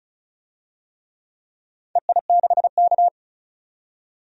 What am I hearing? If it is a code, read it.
Morse code, heard as a keyed tone, about 36 wpm, 700 Hz. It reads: EI6K
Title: DXCC-100 Call Signs - 4 Characters - 35wpm